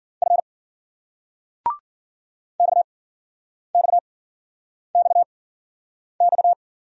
Morse code, heard as a keyed tone, about 50 wpm, spaced as for 9 wpm, 700 Hz. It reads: F XXXX